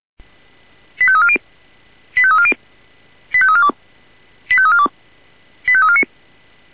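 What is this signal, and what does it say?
Five-tone selective call (ZVEI), heard as quick runs of five tones, tones of about 70 ms. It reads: 85340 85340 85342 85342 85349
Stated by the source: Five-tone ZVEI selective call received on SDR.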